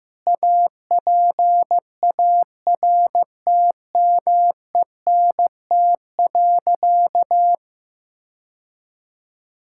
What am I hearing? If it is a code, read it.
Morse code, heard as a keyed tone, about 15 wpm, 700 Hz. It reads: APARTMENT.
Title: Sentences from Top 1000 Words - Review Part 6 - 15wpm